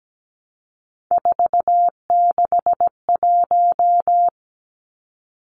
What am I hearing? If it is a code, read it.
Morse code, heard as a keyed tone, about 17 wpm, 700 Hz. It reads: 461
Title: Numbers - 3-Digit 17wpm